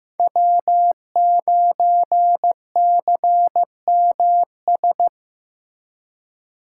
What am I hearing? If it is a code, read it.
Morse code, heard as a keyed tone, about 15 wpm, 700 Hz. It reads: W9CMS
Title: USA Call Signs - 5 Characters - 2x Character Spacing - 15wpm